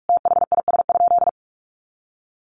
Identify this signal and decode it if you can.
Morse code, heard as a keyed tone, about 44 wpm, 700 Hz. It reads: THIS?
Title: Sentences from Top 1100 Words 45wpm